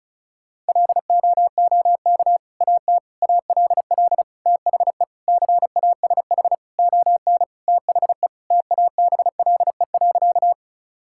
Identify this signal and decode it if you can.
Morse code, heard as a keyed tone, about 36 wpm, 700 Hz. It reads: LOOK AT ALL THE CASH ON THE TABLE.
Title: Sentences from Top 1000 Words - Review Part 5 - 35wpm